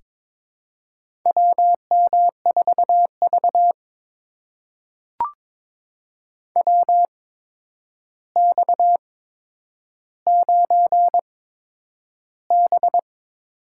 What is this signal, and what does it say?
Morse code, heard as a keyed tone, about 21 wpm, spaced as for 9 wpm, 700 Hz. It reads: WM4V WX9B
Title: USA Call Signs - 4 Characters - 8x Character Spacing - 22wpm